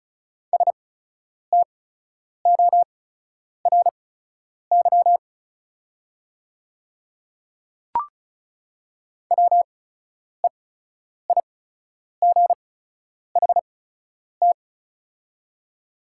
Morse code, heard as a keyed tone, about 36 wpm, spaced as for 10 wpm, 700 Hz. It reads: STORY WEIGHT
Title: 8x Character Spacing - Top 500 Words - 35wpm